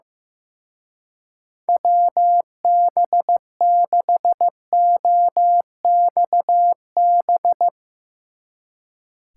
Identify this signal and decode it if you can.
Morse code, heard as a keyed tone, about 16 wpm, 700 Hz. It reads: WB6OXB
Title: Speed Racing - USA 2x3 Call Signs 15wpm